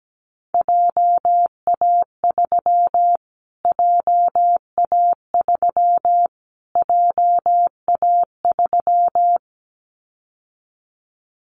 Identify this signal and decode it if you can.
Morse code, heard as a keyed tone, about 17 wpm, 700 Hz. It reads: JA3 JA3 JA3
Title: Callsign Prefixes for Top 25 DXCC - Repeated 3x - 17wpm